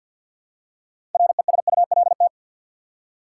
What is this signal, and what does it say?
Morse code, heard as a keyed tone, about 53 wpm, 700 Hz. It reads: RESULT
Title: Six Letter Words - 50wpm